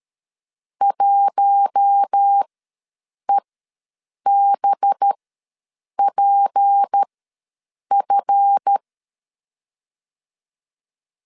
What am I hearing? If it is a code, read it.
Morse code, heard as a keyed tone, about 13 wpm, spaced as for 7 wpm, 800 Hz. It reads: 1EBPF
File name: LEZIONI / 8 YJ45 / 8 YJ4 ws 12wpm 3249